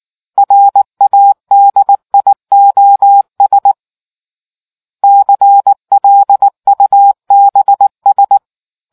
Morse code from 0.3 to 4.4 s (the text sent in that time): RADIOS